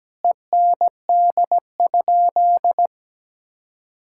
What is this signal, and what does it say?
Morse code, heard as a keyed tone, about 17 wpm, 700 Hz. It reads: END?
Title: Sentences from Top 1000 Words - Review Part 5 - 17wpm